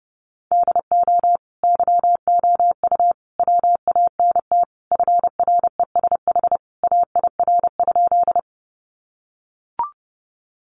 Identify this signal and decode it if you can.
Morse code, heard as a keyed tone, about 30 wpm, 700 Hz. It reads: DO YOU WANT FRESH AIR?
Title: Sentences from Top 1200 Words 30wpm